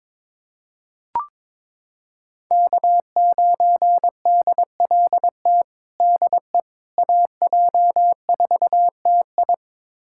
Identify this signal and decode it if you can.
Morse code, heard as a keyed tone, about 22 wpm, 700 Hz. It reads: K9DLT DE AJ4TI
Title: USA Callsign DE Callsign - 5 Characters - 22wpm